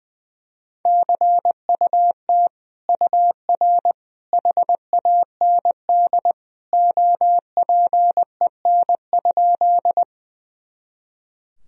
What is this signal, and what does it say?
Morse code, heard as a keyed tone, about 20 wpm, 700 Hz. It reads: CUT UR HAND OPEN?
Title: Sentences from Top 300 Words and Abbreviations 20wpm